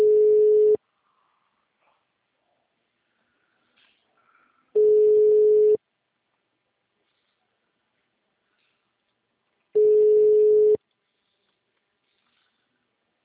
Ringback tone (Europe, 425 Hz).